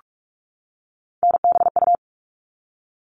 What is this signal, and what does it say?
Morse code, heard as a keyed tone, about 44 wpm, 700 Hz. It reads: NBU
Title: All Cut Numbers - 3-Digit 45wpm